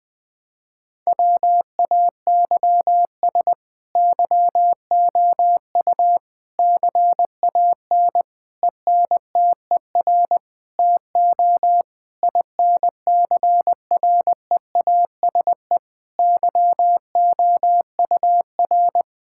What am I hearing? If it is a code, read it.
Morse code, heard as a keyed tone, about 21 wpm, 700 Hz. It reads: WAYS YOU CAN ENTER TO INCREASE YOUR
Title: Sets of 8 Words - 20wpm